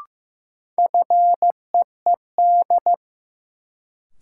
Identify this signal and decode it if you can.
Morse code, heard as a keyed tone, about 16 wpm, 700 Hz. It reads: FEED
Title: Sentences from Top 1600 Words - New Word Drill - 15wpm